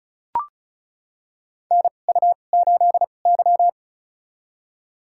Morse code, heard as a keyed tone, about 36 wpm, spaced as for 25 wpm, 700 Hz. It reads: NU8Y
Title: USA Call Signs - 4 Characters - 2x Character Spacing - 35wpm